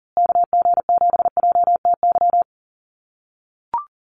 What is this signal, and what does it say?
Morse code, heard as a keyed tone, about 40 wpm, 700 Hz. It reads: KG7JTY
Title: USA Call Signs Repeated 2x 40wpm